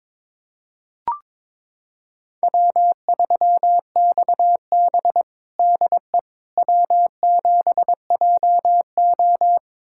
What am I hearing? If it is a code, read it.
Morse code, heard as a keyed tone, about 23 wpm, 700 Hz. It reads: W3XB DE W7JO
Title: USA Callsign DE Callsign - 4 Characters - 22wpm